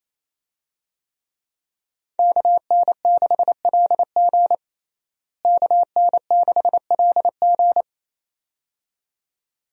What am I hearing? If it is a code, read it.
Morse code, heard as a keyed tone, about 28 wpm, 700 Hz. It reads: KN6LG KN6LG
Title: USA Call Signs - 5 Characters - Repeated 2x - 28wpm